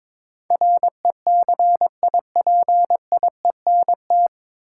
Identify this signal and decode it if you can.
Morse code, heard as a keyed tone, about 23 wpm, 700 Hz. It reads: RECIPIENT
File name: 9 Letter Words - First 3 Letters Given - 22wpm